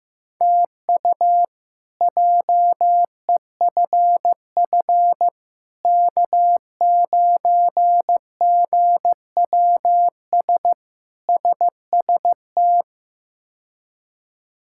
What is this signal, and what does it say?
Morse code, heard as a keyed tone, about 15 wpm, 700 Hz. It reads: TU JEFF K9GWS SST
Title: K1USN SST QSO Practice - 15wpm